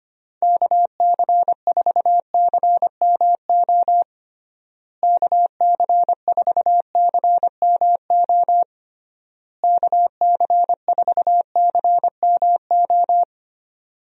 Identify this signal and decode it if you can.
Morse code, heard as a keyed tone, about 25 wpm, 700 Hz. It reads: KC4CMO KC4CMO KC4CMO
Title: USA Call Signs - 6 Characters - Repeated 3x - 25wpm